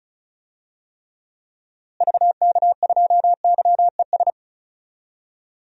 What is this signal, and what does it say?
Morse code, heard as a keyed tone, about 36 wpm, 700 Hz. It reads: VK2YES
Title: DXCC-100 Call Signs - 6 Characters - Repeated 2x - 35wpm